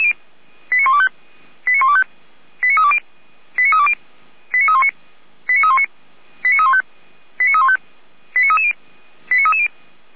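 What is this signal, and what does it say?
Five-tone selective call (ZVEI), heard as quick runs of five tones, tones of about 70 ms. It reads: 89125 89125 89320 89320 89319 89319 89315 89315 89330 89330
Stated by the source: ZVEI five-tone selective call 89050, received on an SDR.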